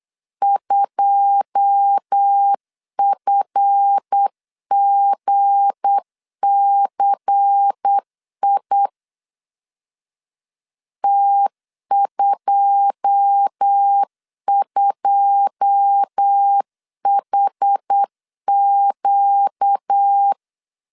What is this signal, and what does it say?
Morse code, heard as a keyed tone, about 8 wpm, 800 Hz. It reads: 2FGCI T22HQ
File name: LEZIONI / 9 2 8wpm 3437